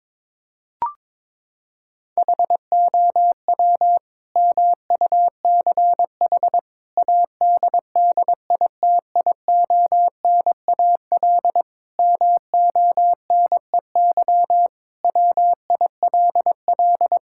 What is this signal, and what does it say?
Morse code, heard as a keyed tone, about 22 wpm, 700 Hz. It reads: HOW MUCH ADDITIONAL MONEY WILL
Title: Sentences from Top 1000 Words - Review Part 5 - 22wpm